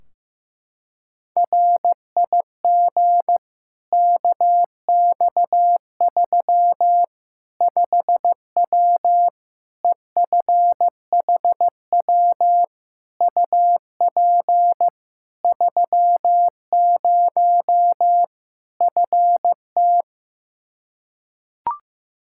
Morse code, heard as a keyed tone, about 15 wpm, 700 Hz. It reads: RIG KX3 5W EFHW UP 30 FT
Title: Practice QSO - Exchange 2 - Info - 15wpm